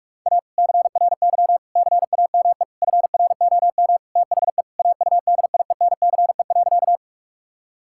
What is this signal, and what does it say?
Morse code, heard as a keyed tone, about 46 wpm, 700 Hz. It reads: A CRY CAME FROM THE AUDIENCE.
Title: Sentences from Top 1000 Words - Review Part 5 - 45wpm